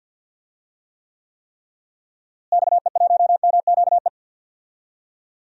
Morse code, heard as a keyed tone, about 52 wpm, 700 Hz. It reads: XE1MQE
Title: DXCC-100 Call Signs - 6 Characters - Repeated 3x - 50wpm